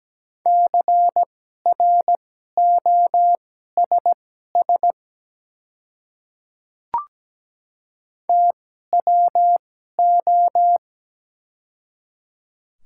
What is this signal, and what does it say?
Morse code, heard as a keyed tone, about 17 wpm, spaced as for 12 wpm, 700 Hz. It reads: CROSS TWO
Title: 2x Character Spacing - Top 300 Words - 17wpm